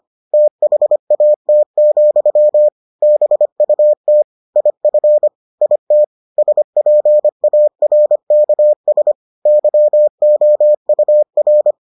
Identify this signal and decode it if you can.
Morse code, heard as a keyed tone, about 25 wpm, 600 Hz. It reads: THAT, BUT IF IT SPARKS YOUR